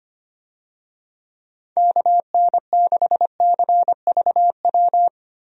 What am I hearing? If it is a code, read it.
Morse code, heard as a keyed tone, about 25 wpm, 700 Hz. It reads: KN6CVW
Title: USA Call Signs - 6 Characters - 25wpm